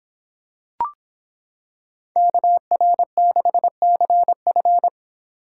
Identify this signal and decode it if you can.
Morse code, heard as a keyed tone, about 25 wpm, 700 Hz. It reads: KR6CF